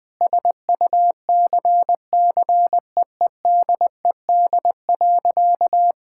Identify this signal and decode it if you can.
Morse code, heard as a keyed tone, about 20 wpm, 700 Hz. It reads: SUCCEEDED.
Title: Sentences with 6-Word Phrases - Spoken Beginning - 20wpm